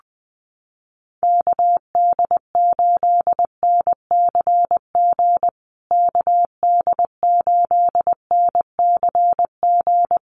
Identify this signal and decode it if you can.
Morse code, heard as a keyed tone, about 20 wpm, 700 Hz. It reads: KD8NCG KD8NCG
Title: USA Call Signs Repeated 3x 20wpm